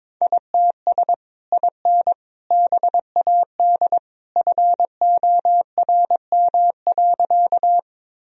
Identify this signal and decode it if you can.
Morse code, heard as a keyed tone, about 22 wpm, 700 Hz. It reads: ITS IN BAD FORM.
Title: Sentences from Top 1000 Words - Review Part 1 - 22wpm